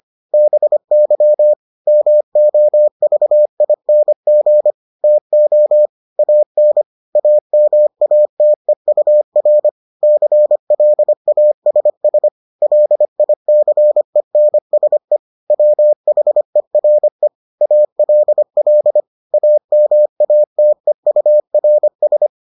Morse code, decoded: BY MOVING TO AN AMATEUR CLASS LICENSE WHERE ALL AMATEURS

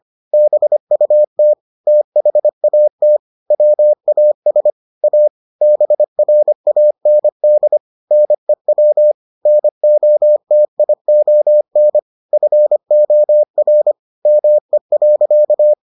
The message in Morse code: BUT THAT WAS A BRAND NEW NOTION FOR ME.

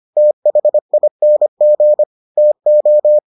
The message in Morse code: THING TO

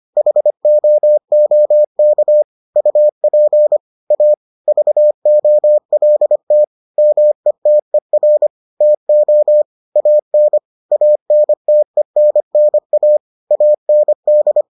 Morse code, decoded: HOOK UP A VOLT METER TO AN ANTENNA AND

